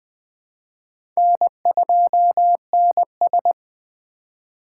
Morse code, heard as a keyed tone, about 20 wpm, 700 Hz. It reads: N2NS
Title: K1USN SST QSO Practice - 20wpm